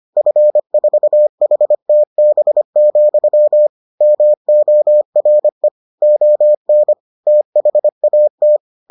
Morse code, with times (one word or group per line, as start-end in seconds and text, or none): 0.2-3.7 F4HTB,
4.0-5.7 MORE
6.0-6.9 ON
7.3-8.6 THAT